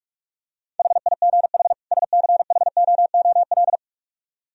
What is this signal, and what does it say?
Morse code, heard as a keyed tone, about 49 wpm, 700 Hz. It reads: HIGH SCHOOL